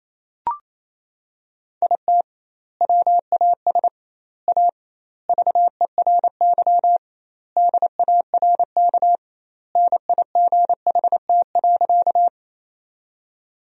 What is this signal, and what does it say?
Morse code, heard as a keyed tone, about 28 wpm, 700 Hz. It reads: IT WAS A VERY DARK NIGHT.
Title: 2x Word-Spacing - Sentences from Top 500 Words - 28wpm